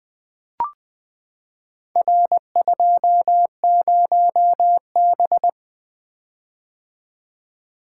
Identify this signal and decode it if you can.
Morse code, heard as a keyed tone, about 20 wpm, 700 Hz. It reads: R20B
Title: Four Letter-Number Combinations - 20wpm